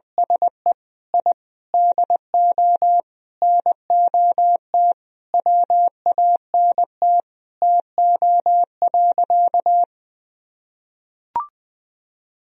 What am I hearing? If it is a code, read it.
Morse code, heard as a keyed tone, about 21 wpm, 700 Hz. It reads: SE I DO NOT WANT TO.
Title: Sentences from Top 1000 Words - Review Part 3 - 20wpm